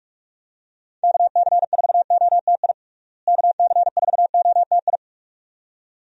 Morse code, heard as a keyed tone, about 46 wpm, 700 Hz. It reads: KC4OTI KC4OTI
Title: USA Call Signs - 6 Characters - Repeated 2x - 45wpm